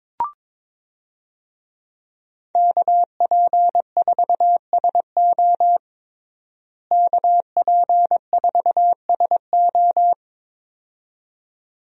Morse code, heard as a keyed tone, about 22 wpm, 700 Hz. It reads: KP4SO KP4SO
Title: DXCC-100 Call Signs - 5 Characters - Repeated 2x - 22wpm